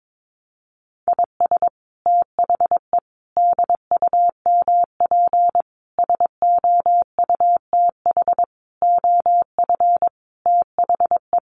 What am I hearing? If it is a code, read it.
Morse code, heard as a keyed tone, about 22 wpm, 700 Hz. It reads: IS THE DUMP SOUTH OF THE
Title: Sentences from Top 1300 Words 22wpm